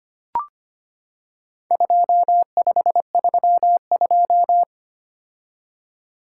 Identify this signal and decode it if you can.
Morse code, heard as a keyed tone, about 25 wpm, 700 Hz. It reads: 2532
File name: practice / 25wpm / Four Letter-Number Combinations - 25wpm